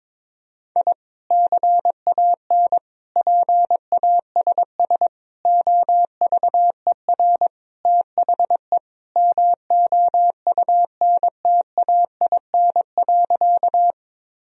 Morse code, heard as a keyed tone, about 21 wpm, 700 Hz. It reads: I CAN PASS OVER THE MOUNTAIN.
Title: Sentences from Top 1000 Words - Review Part 1 - 22wpm